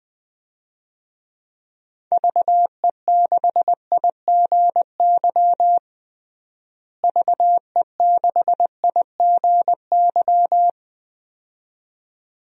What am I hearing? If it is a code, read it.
Morse code, heard as a keyed tone, about 20 wpm, 700 Hz. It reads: VE6IGY VE6IGY
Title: DXCC-100 Call Signs - 6 Characters - Repeated 2x - 20wpm